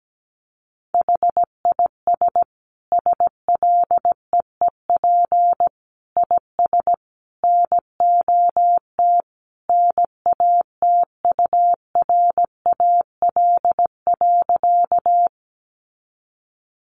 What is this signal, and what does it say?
Morse code, heard as a keyed tone, about 17 wpm, 700 Hz. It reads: HIS SLEEP IS NOT NATURAL.
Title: Sentences from Top 700 Words 17wpm